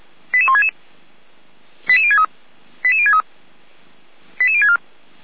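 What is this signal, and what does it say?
Five-tone selective call (ZVEI), heard as quick runs of five tones, tones of about 80 ms. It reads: 88166 88063 88063 88064
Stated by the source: ZVEI five-tone call 88352 received by SDR.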